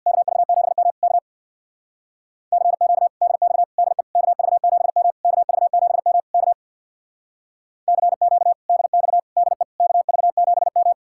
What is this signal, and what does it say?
Morse code, heard as a keyed tone, about 62 wpm, 700 Hz. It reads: KV7M K CQ DX DE KV7M KV7M K CQ DX DE KV7M
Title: Speed-Racing Common QSO Phrases Encore - 50wpm